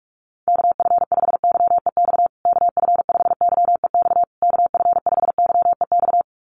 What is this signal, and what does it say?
Morse code, heard as a keyed tone, about 44 wpm, 690 Hz. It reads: KF5YEX KF5YEX KF5YEX
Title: USA Call Signs Repeated 3x 45wpm